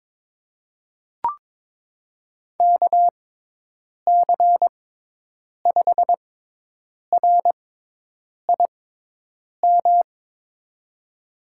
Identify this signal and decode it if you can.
Morse code, heard as a keyed tone, about 22 wpm, spaced as for 8 wpm, 700 Hz. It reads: KC5RIM
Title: USA Call Signs - 6x Character Spacing - 22wpm